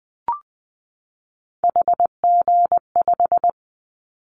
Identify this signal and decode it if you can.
Morse code, heard as a keyed tone, about 20 wpm, 700 Hz. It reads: HG5
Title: Callsign Prefixes for Top 25 DXCC - 20wpm